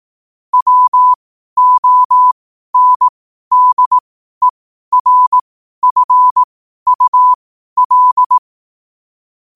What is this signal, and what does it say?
Morse code, heard as a keyed tone, about 18 wpm, spaced as for 12 wpm, 1000 Hz. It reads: WONDERFUL
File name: qotdfiles / 12wpm